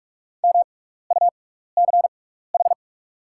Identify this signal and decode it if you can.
Morse code, heard as a keyed tone, about 43 wpm, spaced as for 15 wpm, 690 Hz. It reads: MUCH
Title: 6x Character Spacing - Top 500 Words - 45wpm